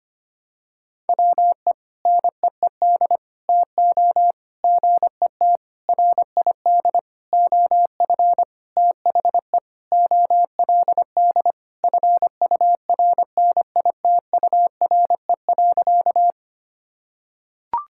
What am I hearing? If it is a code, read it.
Morse code, heard as a keyed tone, about 25 wpm, 700 Hz. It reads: WE NEED TO GET RID OF THE OLD FURNITURE.
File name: practice / 25wpm / Sentences with 4-Word Phrases - 25wpm